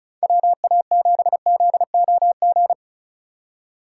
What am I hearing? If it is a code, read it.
Morse code, heard as a keyed tone, about 34 wpm, 690 Hz. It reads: WA7ZOG
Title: USA Call Signs - 6 Characters - 4x Character Spacing - 35wpm